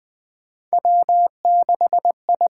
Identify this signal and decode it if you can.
Morse code, heard as a keyed tone, about 20 wpm, 700 Hz. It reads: W6I
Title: USA Call Signs - 4 Characters - 8x Character Spacing - 20wpm